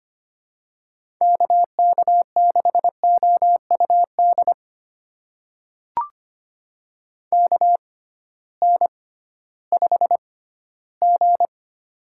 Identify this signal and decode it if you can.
Morse code, heard as a keyed tone, about 25 wpm, spaced as for 15 wpm, 700 Hz. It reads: KK6OUD KN5G
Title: USA Call Signs - 6x Character Spacing - 25wpm